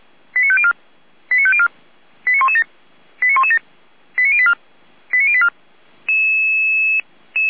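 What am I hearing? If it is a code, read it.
Five-tone selective call (ZVEI), heard as quick runs of five tones, tones of about 70 ms. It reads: 89504 89504 89117 89117 89084 89084
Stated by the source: ZVEI five-tone selective call received by SDR, feuerwehr-orlamuende.org fire brigade server.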